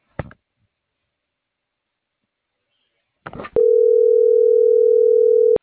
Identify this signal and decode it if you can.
Ringback tone (North America), 440 + 480 Hz.